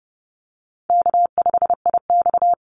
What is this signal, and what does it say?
Morse code, heard as a keyed tone, about 30 wpm, 700 Hz. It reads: K5IX
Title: Call Signs Repeated 3x - 30wpm